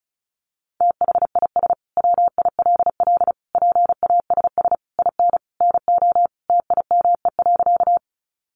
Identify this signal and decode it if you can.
Morse code, heard as a keyed tone, about 34 wpm, 700 Hz. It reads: THIS WILL PASS IN NO TIME.
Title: Sentences from Top 400 Words 35wpm